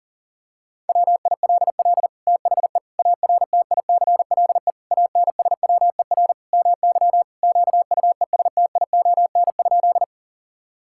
Morse code, heard as a keyed tone, about 40 wpm, 700 Hz. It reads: WILL THE ARTICLE ANSWER MY QUESTION?
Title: Sentences from Top 1000 Words - Review Part 5 - 40wpm